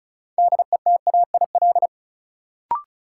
Morse code, heard as a keyed tone, about 36 wpm, 700 Hz. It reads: DETAIL
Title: Sentences from Top 1500 Words - New Word Drill Alt - 35wpm